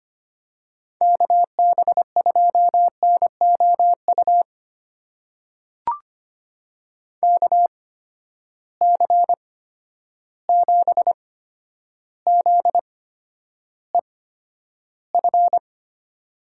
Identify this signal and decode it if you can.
Morse code, heard as a keyed tone, about 25 wpm, spaced as for 11 wpm, 700 Hz. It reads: KB2NOU KC7ZEF
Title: USA Call Signs - 8x Character Spacing - 25wpm